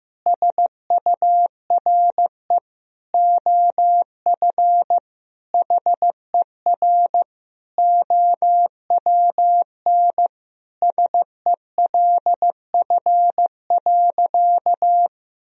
Morse code, heard as a keyed tone, about 15 wpm, 700 Hz. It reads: SURE OF HER OWN SELF.
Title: Sentences from Top 1000 Words - Review Part 1 - 15wpm